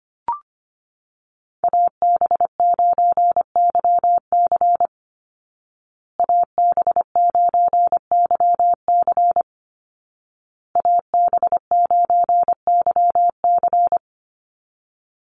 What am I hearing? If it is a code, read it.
Morse code, heard as a keyed tone, about 25 wpm, 700 Hz. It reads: AB9YC AB9YC AB9YC
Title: Call Signs Repeated 2x - 25wpm